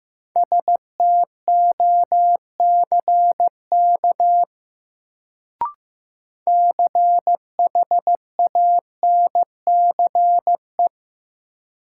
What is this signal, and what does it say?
Morse code, heard as a keyed tone, about 15 wpm, 700 Hz. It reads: STOCK CHANCE
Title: Sentences from Top 800 Words - New Word Drill - 15wpm